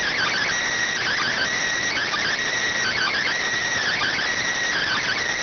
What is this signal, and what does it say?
Emergency Alert System (SAME) tones (header bursts), heard as harsh buzzing bursts. It is heard from 1.0 to 5.2 s.